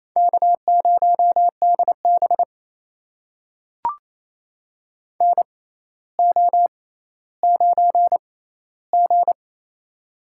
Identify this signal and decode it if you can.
Morse code, heard as a keyed tone, about 28 wpm, spaced as for 14 wpm, 700 Hz. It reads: K0DB NO9G
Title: USA Call Signs - 4 Characters - 6x Character Spacing - 28wpm